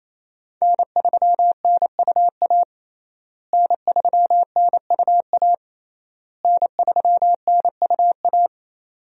Morse code, heard as a keyed tone, about 28 wpm, 700 Hz. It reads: N3NUA N3NUA N3NUA
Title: USA Call Signs - 5 Characters - Repeated 3x - 28wpm